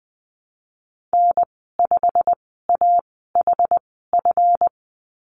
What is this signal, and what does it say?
Morse code, heard as a keyed tone, about 20 wpm, spaced as for 15 wpm, 700 Hz. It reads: N5AHF
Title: 154 Lesson 25g - Letter p - Call Signs 20wpm